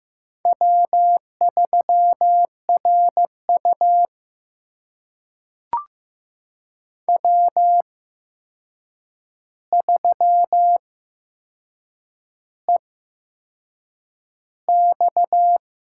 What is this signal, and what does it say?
Morse code, heard as a keyed tone, about 15 wpm, spaced as for 6 wpm, 700 Hz. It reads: W3RU W3EX
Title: USA Call Signs - 8x Character Spacing - 15wpm